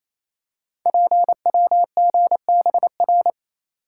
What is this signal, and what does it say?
Morse code, heard as a keyed tone, about 27 wpm, 700 Hz. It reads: PWGBR
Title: Code Groups - 5 Letters - 28wpm